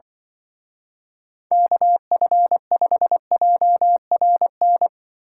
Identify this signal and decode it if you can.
Morse code, heard as a keyed tone, about 25 wpm, 700 Hz. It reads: KF5JRN
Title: Speed Racing - USA 2x3 Call Signs 20wpm